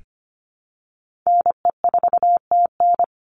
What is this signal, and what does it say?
Morse code, heard as a keyed tone, about 25 wpm, 700 Hz. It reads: NE4TN
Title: NA SOTA Chaser Callsigns 25wpm